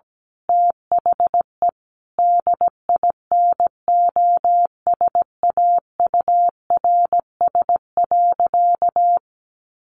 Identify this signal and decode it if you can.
Morse code, heard as a keyed tone, about 17 wpm, 700 Hz. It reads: THE DINOSAURS.
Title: A Sound Of Thunder 17wpm